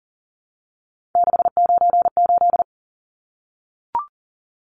Morse code, heard as a keyed tone, about 40 wpm, 700 Hz. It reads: 698
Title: Numbers - 3-Digit 40wpm